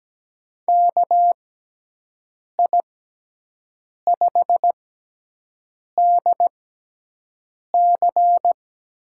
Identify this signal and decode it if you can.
Morse code, heard as a keyed tone, about 17 wpm, 700 Hz. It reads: K I 5 D C
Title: USA Call Signs - 6 Characters - 6x Character Spacing - 17wpm